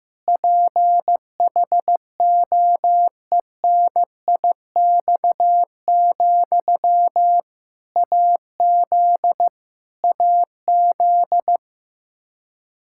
Morse code, heard as a keyed tone, about 15 wpm, 700 Hz. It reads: PHOENIX, AZ AZ
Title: K1USN SST QSO Practice 15wpm